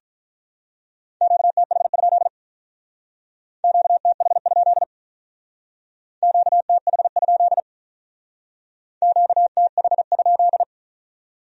Morse code, heard as a keyed tone, about 45 wpm, 700 Hz. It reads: QTH? QTH? QTH? QTH?